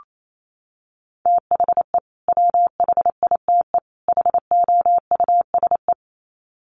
Morse code, heard as a keyed tone, about 28 wpm, 700 Hz. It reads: THE WHITE HOUSE
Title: Sets of 3 Words 28wpm